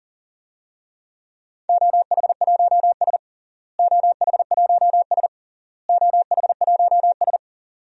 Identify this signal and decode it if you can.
Morse code, heard as a keyed tone, about 40 wpm, 700 Hz. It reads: OH1S OH1S OH1S
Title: DXCC-100 Call Signs - 4 Characters - Repeated 3x - 40wpm